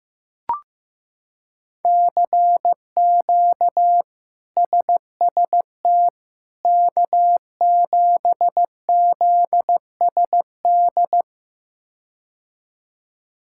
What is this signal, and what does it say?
Morse code, heard as a keyed tone, about 15 wpm, 700 Hz. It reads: CQ SST K7ZSD
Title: K1USN SST QSO Practice - 15wpm